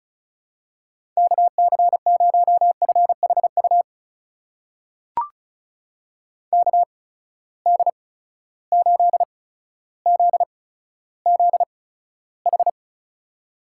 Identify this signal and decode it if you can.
Morse code, heard as a keyed tone, about 36 wpm, spaced as for 15 wpm, 700 Hz. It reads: KC0FHU KD8ZZH
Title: USA Call Signs - 6 Characters - 8x Character Spacing - 35wpm